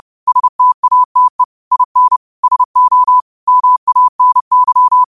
Morse code, heard as a keyed tone, about 30 wpm, 1000 Hz. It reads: STATE IN SO MANY
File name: qotdfiles / 30wpm